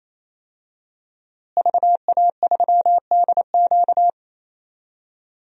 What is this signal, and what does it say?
Morse code, heard as a keyed tone, about 27 wpm, 700 Hz. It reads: VA3DQ
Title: DXCC-100 Call Signs - 5 Characters - Repeated 3x - 28wpm